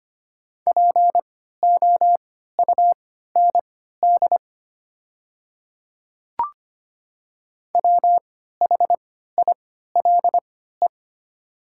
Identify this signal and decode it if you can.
Morse code, heard as a keyed tone, about 25 wpm, spaced as for 14 wpm, 700 Hz. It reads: POUND WHILE
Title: 3x Character Spacing - Top 500 Words - 25wpm